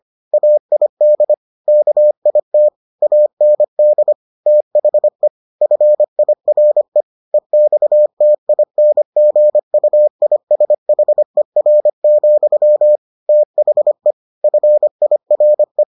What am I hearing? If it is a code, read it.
Morse code, heard as a keyed tone, about 25 wpm, 600 Hz. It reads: AID KIT AND THE FIRE EXTINGUISHER, THE FIRE